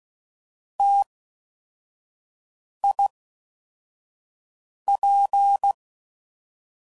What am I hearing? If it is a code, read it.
Morse code, heard as a keyed tone, about 16 wpm, 800 Hz. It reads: T I P
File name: Combinations / Difficult Combos / T AOEI's